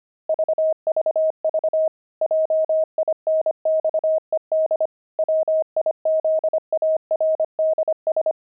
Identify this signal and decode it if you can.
Morse code, heard as a keyed tone, about 25 wpm, 630 Hz. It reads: VVV JINXED WIZARDS
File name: pangram-fs-25-cs-25-jinxed